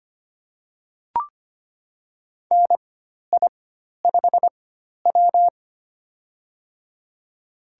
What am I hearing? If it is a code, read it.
Morse code, heard as a keyed tone, about 25 wpm, spaced as for 12 wpm, 700 Hz. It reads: NI5W
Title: USA Call Signs - 4 Characters - 4x Character Spacing - 25wpm